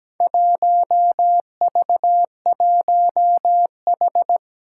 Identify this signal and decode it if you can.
Morse code, heard as a keyed tone, about 17 wpm, 700 Hz. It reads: 1V1H
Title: Four Challenging Letter-Number Combinations 17wpm